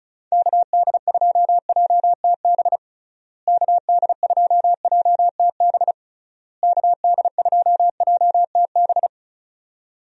Morse code, heard as a keyed tone, about 36 wpm, 700 Hz. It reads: KD2JTB KD2JTB KD2JTB
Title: USA Call Signs - 6 Characters - Repeated 3x - 35wpm